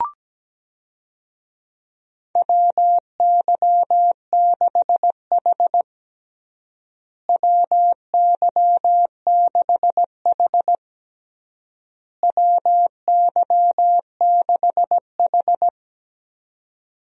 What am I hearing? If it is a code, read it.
Morse code, heard as a keyed tone, about 17 wpm, 700 Hz. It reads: WY6H WY6H WY6H
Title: USA Call Signs - 4 Characters - Repeated 3x - 17wpm